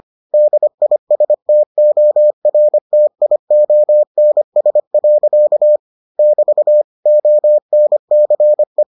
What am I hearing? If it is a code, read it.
Morse code, heard as a keyed tone, about 25 wpm, 600 Hz. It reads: DISTORTIONS. <BT> ONCE